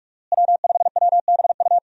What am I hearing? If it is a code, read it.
Morse code, heard as a keyed tone, about 45 wpm, 700 Hz. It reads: W5WBU
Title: USA Call Signs - 5 Characters - Repeated 2x - 45wpm